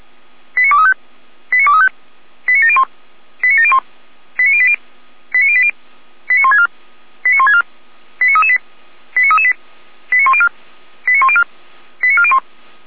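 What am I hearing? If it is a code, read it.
Five-tone selective call (ZVEI), heard as quick runs of five tones, tones of about 70 ms. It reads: 89235 89235 89701 89701 89080 89080 89164 89164 89338 89338 89104 89104 89401
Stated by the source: ZVEI five-tone selective call received by SDR, Feuerwehr Orlamünde.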